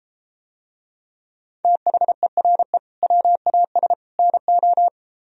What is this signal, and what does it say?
Morse code, heard as a keyed tone, about 32 wpm, 700 Hz. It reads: THERE WAS NO